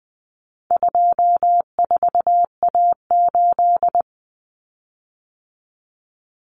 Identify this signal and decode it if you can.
Morse code, heard as a keyed tone, about 19 wpm, 700 Hz. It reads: 24A8
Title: Common Cut Numbers - 4-Digit 20wpm